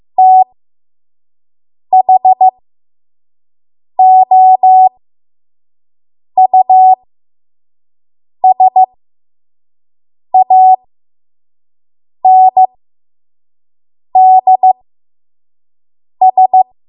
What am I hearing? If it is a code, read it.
Morse code, heard as a keyed tone, about 15 wpm, spaced as for 5 wpm, 750 Hz. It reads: THOUSANDS
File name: Morse / 05WPM